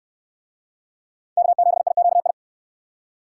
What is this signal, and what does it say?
Morse code, heard as a keyed tone, about 63 wpm, 700 Hz. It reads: K7IQI